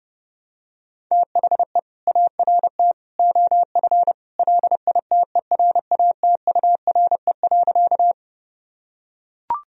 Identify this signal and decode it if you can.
Morse code, heard as a keyed tone, about 30 wpm, 700 Hz. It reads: THE ART OF LITERATURE.
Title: Sentences from Top 1000 Words - Review Part 3 - 30wpm